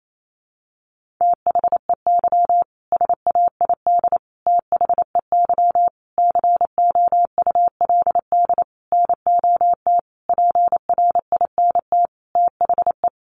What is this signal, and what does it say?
Morse code, heard as a keyed tone, about 28 wpm, 700 Hz. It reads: THEY SAID THEY COULD NOT PRINT THE
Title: Sentences from Top 1400 Words 28wpm